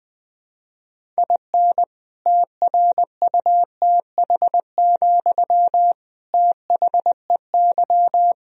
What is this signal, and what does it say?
Morse code, heard as a keyed tone, about 20 wpm, 700 Hz. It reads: IN TRUTH, THEY
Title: Sentences from Top 1000 Words - Review Part 1 - 20wpm